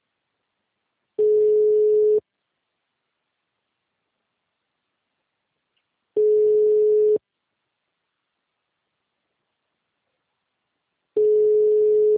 Ringback tone (Europe, 425 Hz).